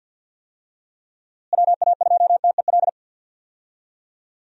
Morse code, heard as a keyed tone, about 50 wpm, 700 Hz. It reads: WA2TEL
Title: USA Call Signs - 6 Characters - Repeated 2x - 50wpm